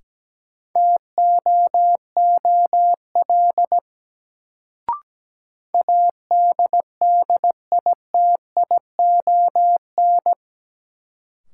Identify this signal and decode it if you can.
Morse code, heard as a keyed tone, about 17 wpm, 700 Hz. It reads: TOOL ADDITION
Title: Sentences from Top 1000 Words - Review Part 6 - New Word Drill - 17wpm